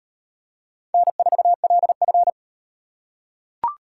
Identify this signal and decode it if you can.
Morse code, heard as a keyed tone, about 35 wpm, 700 Hz. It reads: N4LF